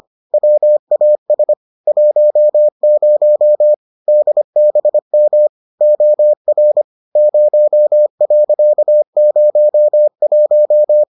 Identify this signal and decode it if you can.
Morse code, heard as a keyed tone, about 25 wpm, 600 Hz. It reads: WAS 10 DBM OR 0.01